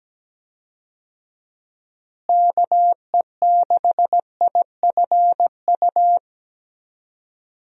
Morse code, heard as a keyed tone, about 17 wpm, 700 Hz. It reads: KE6IFU
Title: USA Call Signs - 6 Characters - Repeated 2x - 17wpm